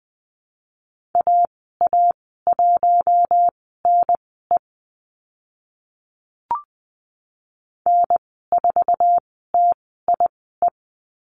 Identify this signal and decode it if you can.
Morse code, heard as a keyed tone, about 20 wpm, spaced as for 15 wpm, 700 Hz. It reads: AA1NE N4TIE
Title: 082 Lesson 14g - Number 2 - Call Signs 20wpm